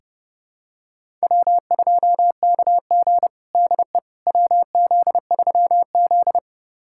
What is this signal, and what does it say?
Morse code, heard as a keyed tone, about 32 wpm, 700 Hz. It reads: W2KG DE WZ3Z
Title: USA Callsign DE Callsign - 4 Characters - 30wpm